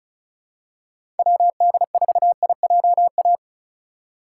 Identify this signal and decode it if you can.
Morse code, heard as a keyed tone, about 34 wpm, 700 Hz. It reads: WD4IJA
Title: USA Call Signs - 6 Characters - 6x Character Spacing - 35wpm